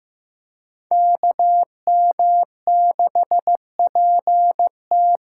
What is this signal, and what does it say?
Morse code, heard as a keyed tone, about 15 wpm, 700 Hz. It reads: KM6PT
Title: USA Call Signs - 6x Character Spacing - 15wpm